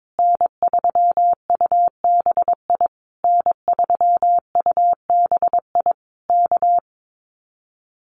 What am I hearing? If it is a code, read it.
Morse code, heard as a keyed tone, about 22 wpm, 700 Hz. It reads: N3UBI N3UBI K
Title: Common QSO Phrases Encore 22wpm